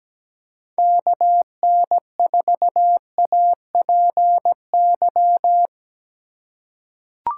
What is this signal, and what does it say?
Morse code, heard as a keyed tone, about 17 wpm, 700 Hz. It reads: KN4APY
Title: USA Call Signs - 6 Characters - 4x Character Spacing - 17wpm